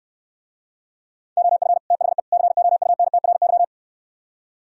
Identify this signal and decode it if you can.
Morse code, heard as a keyed tone, about 68 wpm, 700 Hz. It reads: OF THE COUNTRY